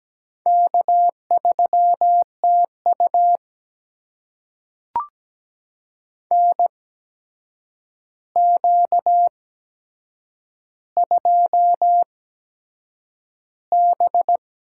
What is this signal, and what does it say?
Morse code, heard as a keyed tone, about 17 wpm, spaced as for 7 wpm, 700 Hz. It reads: K3TU NQ2B
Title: USA Call Signs - 8x Character Spacing - 17wpm